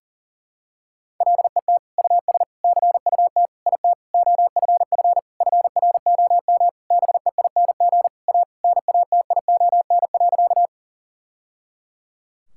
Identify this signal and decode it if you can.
Morse code, heard as a keyed tone, about 40 wpm, 700 Hz. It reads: LET US CUT IT OFF FROM BEING A NATION.
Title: Sentences from Top 1000 Words - Review Part 6 - 40wpm